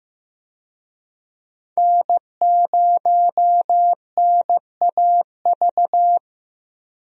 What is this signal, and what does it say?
Morse code, heard as a keyed tone, about 15 wpm, 700 Hz. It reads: N0NAV
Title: USA Call Signs - 5 Characters - Repeated 3x - 15wpm